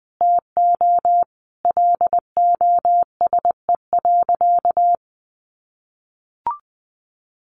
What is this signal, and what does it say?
Morse code, heard as a keyed tone, about 20 wpm, 700 Hz. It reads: TO LOSE.
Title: Sentences from Top 1100 Words 20wpm